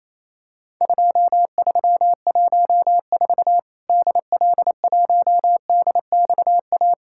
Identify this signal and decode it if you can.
Morse code, heard as a keyed tone, about 28 wpm, 700 Hz. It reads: 2314 DL1DXA
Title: Worked All Europe DX Contest - QTC Exchange - 28wpm